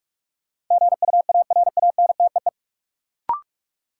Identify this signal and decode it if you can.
Morse code, heard as a keyed tone, about 42 wpm, 700 Hz. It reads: GUARANTEE
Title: Sentences from Top 1600 Words - New Word Drill Alt - 45wpm